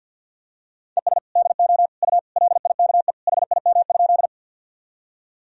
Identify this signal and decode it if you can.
Morse code, heard as a keyed tone, about 52 wpm, 700 Hz. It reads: ES DO U LIKE HIM?
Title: Sentences from Top 100 Words and Abbreviations 50wpm